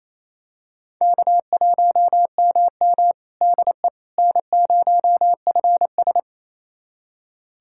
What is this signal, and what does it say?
Morse code, heard as a keyed tone, about 27 wpm, 700 Hz. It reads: K1MM DE N0FS
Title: USA Callsign DE Callsign - 4 Characters - 28wpm